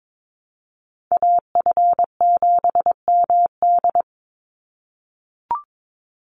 Morse code, heard as a keyed tone, about 22 wpm, 700 Hz. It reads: AF7MD
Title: NA SOTA Chaser Callsigns 22wpm